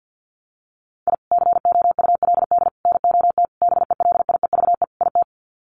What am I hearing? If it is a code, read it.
Morse code, heard as a keyed tone, about 50 wpm, 700 Hz. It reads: I COULD NOT BELIEVE IT
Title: Sets of 5 Words Encore 50wpm